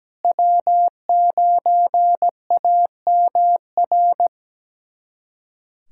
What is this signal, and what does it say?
Morse code, heard as a keyed tone, about 17 wpm, 700 Hz. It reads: W9AMR
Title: USA Callsign DE Callsign - 5 Characters - 17wpm